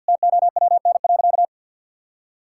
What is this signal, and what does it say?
Morse code, heard as a keyed tone, about 52 wpm, 700 Hz. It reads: TOWN.
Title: Sentences from Top 1000 Words - Review Part 6 - 50wpm